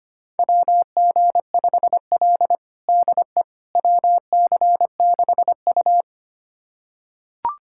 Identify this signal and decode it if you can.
Morse code, heard as a keyed tone, about 25 wpm, 700 Hz. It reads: WG5L DE WC6U
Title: USA Callsign DE Callsign - 4 Characters - 25wpm